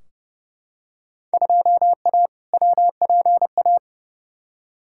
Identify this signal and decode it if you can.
Morse code, heard as a keyed tone, about 32 wpm, 700 Hz. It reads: 2A WPA
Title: Field Day Exchanges - 30wpm